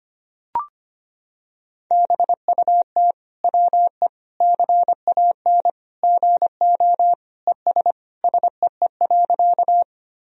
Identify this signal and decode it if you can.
Morse code, heard as a keyed tone, about 25 wpm, 700 Hz. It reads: BUT WE CAN GO ES SEE.
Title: Sentences from Top 100 Words and Abbreviations 25wpm